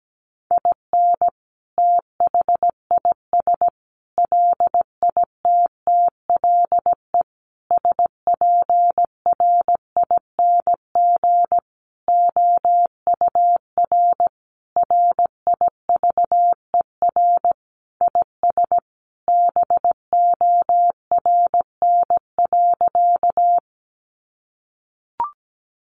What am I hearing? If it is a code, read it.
Morse code, heard as a keyed tone, about 17 wpm, 700 Hz. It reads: IN THIS LITTLE SPRING OUR RIVER IS BORN.
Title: Sentences from Top 1200 Words 17wpm